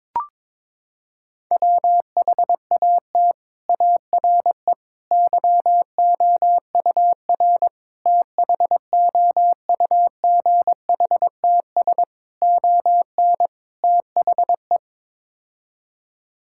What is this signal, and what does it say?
Morse code, heard as a keyed tone, about 22 wpm, 700 Hz. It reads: WHAT ARE YOUR THOUGHTS ON THE
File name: practice / 22wpm / Sets of 6 Words - 22wpm